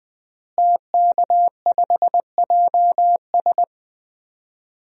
Morse code, heard as a keyed tone, about 20 wpm, 700 Hz. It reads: TK5JS
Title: DXCC-100 Call Signs - 5 Characters - Repeated 3x - 20wpm